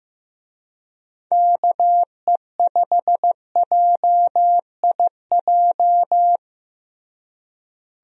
Morse code, heard as a keyed tone, about 15 wpm, 700 Hz. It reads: KE5JIJ
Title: USA Call Signs - 6 Characters - Repeated 3x - 15wpm